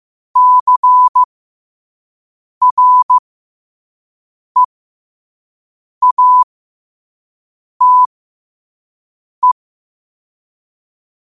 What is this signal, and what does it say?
Morse code, heard as a keyed tone, about 15 wpm, spaced as for 5 wpm, 1000 Hz. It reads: CREATE
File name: qotdfiles / 5wpm